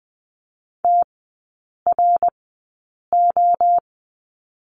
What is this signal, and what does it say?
Morse code, heard as a keyed tone, about 20 wpm, spaced as for 8 wpm, 700 Hz. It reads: TRO
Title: 053 Lesson 10e - Letter r - Three Characters 20wpm